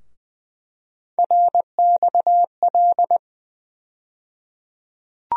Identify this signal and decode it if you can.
Morse code, heard as a keyed tone, about 20 wpm, 700 Hz. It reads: RXL
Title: USA Call Sign Postfixes - 3 Letters 20wpm